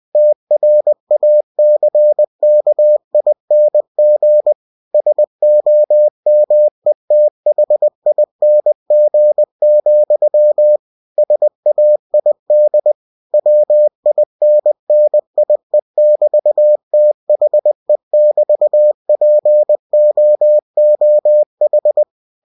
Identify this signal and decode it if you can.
Morse code, heard as a keyed tone, about 20 wpm, 600 Hz. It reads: TRACKING SOMETHING, SAID WINNIE<BT>THE<BT>POOH
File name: winnie-20wpm